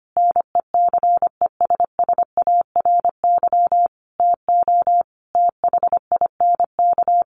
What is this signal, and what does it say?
Morse code, heard as a keyed tone, about 25 wpm, 700 Hz. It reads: NECESSARY TO THINK
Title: Sentences from Top 1300 Words 25wpm